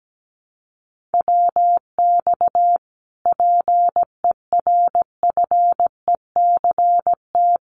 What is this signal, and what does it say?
Morse code, heard as a keyed tone, about 17 wpm, 700 Hz. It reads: WX PERFECT
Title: Common QSO Phrases Encore 17wpm